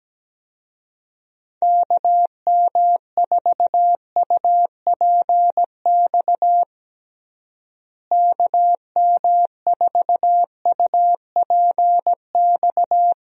Morse code, heard as a keyed tone, about 17 wpm, 700 Hz. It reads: KM4UPX KM4UPX
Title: USA Call Signs - 6 Characters - Repeated 2x - 17wpm